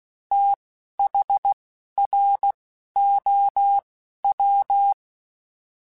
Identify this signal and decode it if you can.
Morse code, heard as a keyed tone, about 16 wpm, spaced as for 11 wpm, 800 Hz. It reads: THROW
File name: TH / TH mp3s / TH Prefixes spt @10